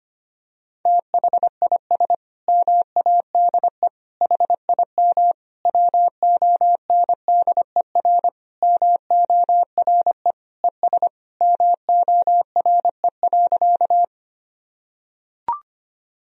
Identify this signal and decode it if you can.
Morse code, heard as a keyed tone, about 25 wpm, 700 Hz. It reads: THIS MADE HIM WONDER MORE ES MORE.
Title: Sentences from Top 500 Words and Abbreviations 25wpm